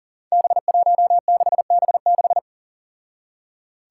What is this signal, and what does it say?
Morse code, heard as a keyed tone, about 40 wpm, 700 Hz. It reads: B16B6
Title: Five Challenging Letter-Number Combinations 40wpm